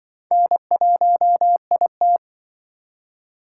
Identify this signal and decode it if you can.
Morse code, heard as a keyed tone, about 26 wpm, 700 Hz. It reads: N1IT